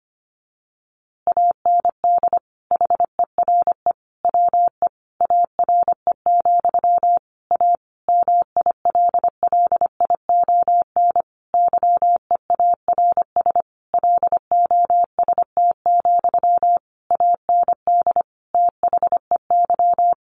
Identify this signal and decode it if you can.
Morse code, heard as a keyed tone, about 25 wpm, 700 Hz. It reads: AND HERE WE ARE, A MILLION YEARS LOST, AND THEY
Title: A Sound Of Thunder 25wpm